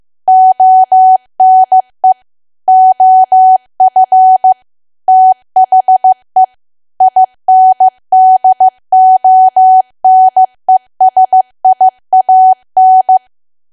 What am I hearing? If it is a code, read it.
Morse code, heard as a keyed tone, about 15 wpm, 750 Hz. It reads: ONE OF THE INDONESIAN
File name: cwa / W1AW-Level2 / 15WPM